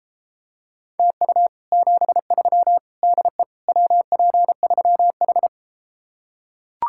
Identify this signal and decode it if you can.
Morse code, heard as a keyed tone, about 33 wpm, 700 Hz. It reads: TU 73 DE WP3H